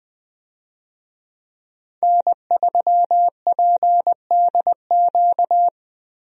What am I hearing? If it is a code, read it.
Morse code, heard as a keyed tone, about 21 wpm, 700 Hz. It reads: N3PDQ
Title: USA Call Signs - 5 Characters - Repeated 2x - 20wpm